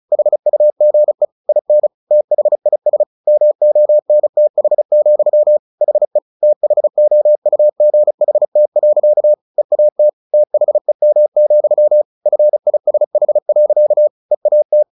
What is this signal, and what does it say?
Morse code, heard as a keyed tone, about 34 wpm, 600 Hz. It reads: HUGE IN THIS MONTH, HE THOUGHT. EAT THEM, FISH. EAT
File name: the old man and the sea / the old man and the sea 1 35wpm